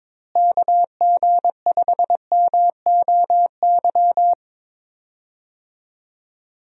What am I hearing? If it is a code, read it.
Morse code, heard as a keyed tone, about 22 wpm, 700 Hz. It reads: KG5MOY
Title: USA Call Signs - 6 Characters - Repeated 2x - 22wpm